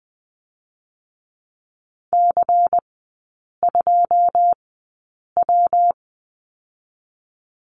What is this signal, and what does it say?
Morse code, heard as a keyed tone, about 20 wpm, 700 Hz. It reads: C 2 W
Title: 114 Lesson 19e - Letter w - Three Characters 20wpm